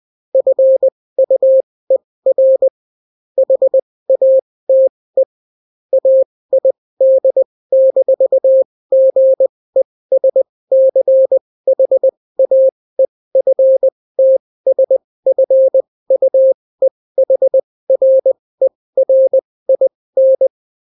Morse code, heard as a keyed tone, about 20 wpm, spaced as for 16 wpm, 530 Hz. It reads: FUER HATE AID-GESCHAEFTSFUEHRERIN